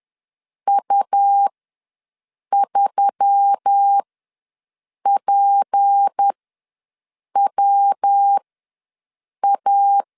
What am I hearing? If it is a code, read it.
Morse code, heard as a keyed tone, about 10 wpm, spaced as for 6 wpm, 800 Hz. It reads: U3PWA
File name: LEZIONI / 6 PKX0 / 6 P ws 10wpm 3751